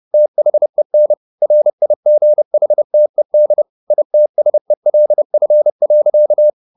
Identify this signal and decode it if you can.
Morse code, heard as a keyed tone, about 30 wpm, 600 Hz. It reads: THEN RIGHTED ITSELF.
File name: the old man and the sea / the old man and the sea 1 30wpm